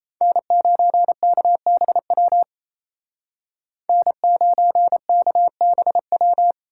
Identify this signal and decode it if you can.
Morse code, heard as a keyed tone, about 33 wpm, 700 Hz. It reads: N9KBW N9KBW
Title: Speed Racing - USA 1x3 Call Signs 28wpm